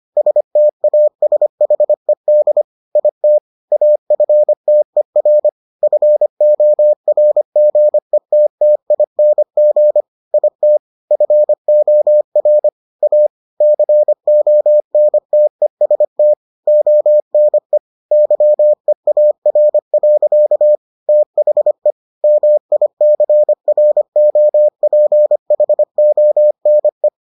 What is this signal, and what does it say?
Morse code, heard as a keyed tone, about 25 wpm, 600 Hz. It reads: STASHED IT AFTER FORGETTING IT FOR A CONTEST ONE YEAR. THE MICROPHONE